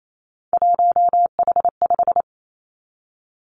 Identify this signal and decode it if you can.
Morse code, heard as a keyed tone, about 28 wpm, 700 Hz. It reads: 1H5
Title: Three Challenging Letter-Number Combinations 28wpm